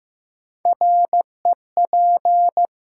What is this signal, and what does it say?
Morse code, heard as a keyed tone, about 15 wpm, 700 Hz. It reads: REP
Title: Sentences from Top 1600 Words - New Word Drill - 15wpm